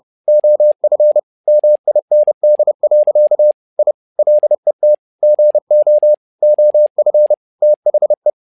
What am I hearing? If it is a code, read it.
Morse code, heard as a keyed tone, about 30 wpm, 600 Hz. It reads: OF MIND. I LET GO OF THE
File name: learning center / tower trouble / tower trouble 30wpm